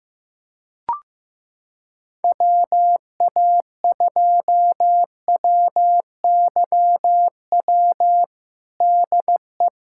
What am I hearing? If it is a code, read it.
Morse code, heard as a keyed tone, about 15 wpm, 700 Hz. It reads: WA2WYW DE
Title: USA Callsign DE Callsign - 6 Characters - 15wpm